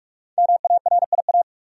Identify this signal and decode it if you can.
Morse code, heard as a keyed tone, about 44 wpm, 700 Hz. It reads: MARIA
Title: USA Common Names - Max 5 Letters - 45wpm